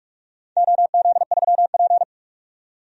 Morse code, heard as a keyed tone, about 46 wpm, 700 Hz. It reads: OZ3P
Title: DXCC-100 Call Signs - 4 Characters - 45wpm